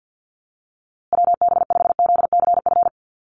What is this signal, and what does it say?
Morse code, heard as a keyed tone, about 50 wpm, 700 Hz. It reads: WB5ZCF
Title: USA Call Signs Repeated 3x 50wpm